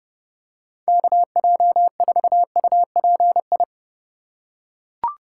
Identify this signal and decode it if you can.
Morse code, heard as a keyed tone, about 30 wpm, 700 Hz. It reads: KJ4UPI